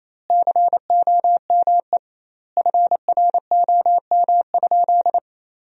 Morse code, heard as a keyed tone, about 28 wpm, 700 Hz. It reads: COME FROM?
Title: 2x Word-Spacing - Sentences from Top 200 Words - 28wpm